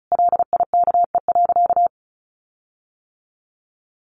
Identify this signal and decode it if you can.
Morse code, heard as a keyed tone, about 34 wpm, 700 Hz. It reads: LIKE.
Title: Sentences from Top 1400 Words 35wpm